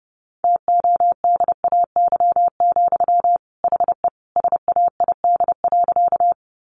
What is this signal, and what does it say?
Morse code, heard as a keyed tone, about 30 wpm, 700 Hz. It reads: TODAY, HE SAID.
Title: Sentences from Top 600 Words 30wpm